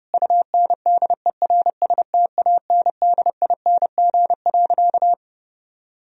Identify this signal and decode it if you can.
Morse code, heard as a keyed tone, about 30 wpm, 700 Hz. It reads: UNDERSTANDING.
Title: Sentences from Top 1000 Words - Review Part 3 - 30wpm